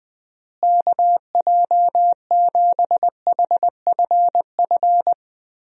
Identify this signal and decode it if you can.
Morse code, heard as a keyed tone, about 21 wpm, 700 Hz. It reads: KJ7HFF
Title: USA Call Signs - 6 Characters - Repeated 3x - 20wpm